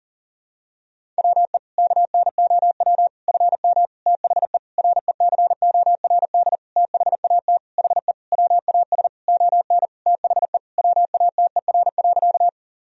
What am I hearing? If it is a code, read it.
Morse code, heard as a keyed tone, about 40 wpm, 700 Hz. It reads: WE KNOW FM THE RECORD THAT HE WAS ON THE WATER.
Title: Sentences from Top 500 Words and Abbreviations 40wpm